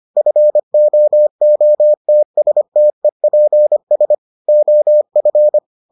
Morse code, heard as a keyed tone, about 25 wpm, 600 Hz. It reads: FOOTSTEPS OF